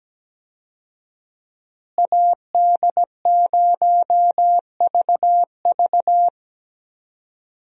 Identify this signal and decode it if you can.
Morse code, heard as a keyed tone, about 17 wpm, 700 Hz. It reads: AD0VV
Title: USA Call Signs - 5 Characters - Repeated 2x - 17wpm